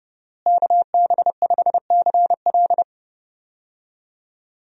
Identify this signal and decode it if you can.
Morse code, heard as a keyed tone, about 30 wpm, 700 Hz. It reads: KB5CL
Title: USA Call Signs - 5 Characters - Repeated 3x - 30wpm